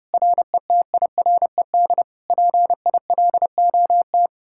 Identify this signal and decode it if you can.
Morse code, heard as a keyed tone, about 40 wpm, 700 Hz. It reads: RETIRED PILOT